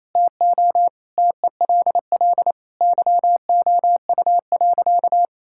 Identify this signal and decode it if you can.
Morse code, heard as a keyed tone, about 28 wpm, 700 Hz. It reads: TO TELL YOU.
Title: Sentences from Top 1000 Words - Review Part 5 - 28wpm